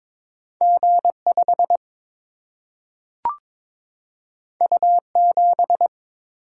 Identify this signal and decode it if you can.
Morse code, heard as a keyed tone, about 22 wpm, 700 Hz. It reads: G 5 U 7
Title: Single-Letter Single-Number 22wpm